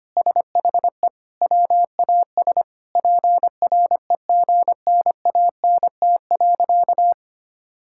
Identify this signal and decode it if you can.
Morse code, heard as a keyed tone, about 25 wpm, 700 Hz. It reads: SHE WAS PREGNANT.
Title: Sentences from Top 1000 Words - Review Part 6 - 25wpm